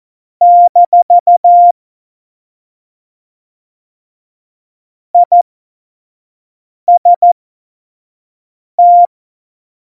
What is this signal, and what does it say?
Morse code, heard as a keyed tone, about 14 wpm, spaced as for 5 wpm, 700 Hz. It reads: - IST
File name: Morsekurs / Witze / Witz 08. August 17, Tempo 25 F70